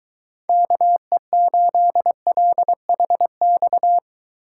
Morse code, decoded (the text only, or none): KE8LHX